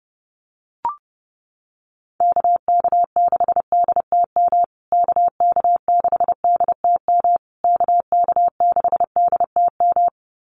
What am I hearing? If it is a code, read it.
Morse code, heard as a keyed tone, about 30 wpm, 700 Hz. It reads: KK6DTM KK6DTM KK6DTM
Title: USA Call Signs Repeated 3x 30wpm